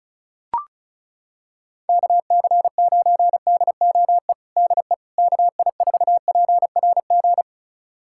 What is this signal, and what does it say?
Morse code, heard as a keyed tone, about 34 wpm, 700 Hz. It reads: KC9DOE DE KI4PRG
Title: USA Callsign DE Callsign - 6 Characters - 35wpm